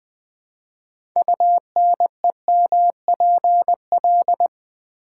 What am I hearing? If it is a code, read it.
Morse code, heard as a keyed tone, about 20 wpm, 700 Hz. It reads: UNEMPL
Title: Build A Word - 6 Letters At A Time - 12-Letter Words - 20wpm